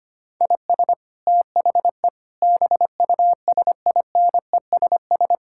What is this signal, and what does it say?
Morse code, heard as a keyed tone, about 25 wpm, 700 Hz. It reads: IS THE BUSINESS
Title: Sentences from Top 1000 Words - Review Part 6 - 25wpm